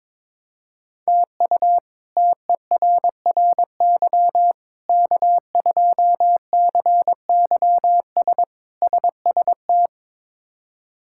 Morse code, heard as a keyed tone, about 22 wpm, 700 Hz. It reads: TU TERRY K2CYS SST
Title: K1USN SST QSO Practice - 22wpm